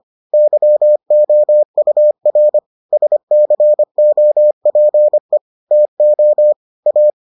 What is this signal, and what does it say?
Morse code, heard as a keyed tone, about 25 wpm, 600 Hz. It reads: YOUR SCOPE TO A